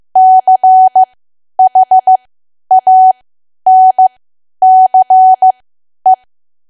Morse code, heard as a keyed tone, about 15 wpm, spaced as for 10 wpm, 750 Hz. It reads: CHANCE
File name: Archive / 10 WPM / 10WPM